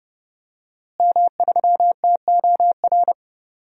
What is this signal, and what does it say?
Morse code, heard as a keyed tone, about 29 wpm, 700 Hz. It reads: M3TOR
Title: DXCC-100 Call Signs - 5 Characters - Repeated 3x - 30wpm